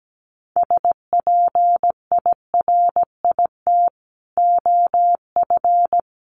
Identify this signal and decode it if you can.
Morse code, heard as a keyed tone, about 17 wpm, 700 Hz. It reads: SPIRIT OF
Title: Sentences from Top 1400 Words 17wpm